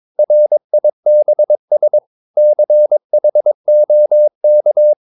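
Morse code, heard as a keyed tone, about 22 wpm, 600 Hz. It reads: RIBS CHOK
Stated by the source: Morse practice of short Danish words at 22 wpm, from CW med Gnister.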